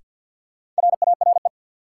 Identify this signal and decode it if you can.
Morse code, heard as a keyed tone, about 51 wpm, 700 Hz. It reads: RARE
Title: Sentences from Top 1100 Words - New Word Drill Alt - 50wpm